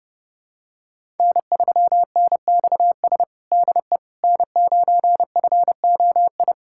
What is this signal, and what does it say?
Morse code, heard as a keyed tone, about 32 wpm, 700 Hz. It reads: N3NXS DE N9FOI
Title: USA Callsign DE Callsign - 5 Characters - 30wpm